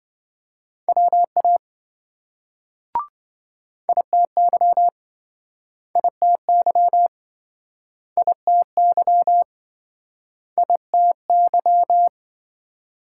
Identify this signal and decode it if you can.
Morse code, heard as a keyed tone, about 26 wpm, 700 Hz. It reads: WA ITY ITY ITY ITY